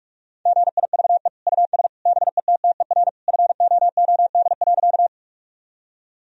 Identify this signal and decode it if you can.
Morse code, heard as a keyed tone, about 45 wpm, 700 Hz. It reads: GIVE US BETTER FOOD.
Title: Sentences from Top 1000 Words - Review Part 1 - 45wpm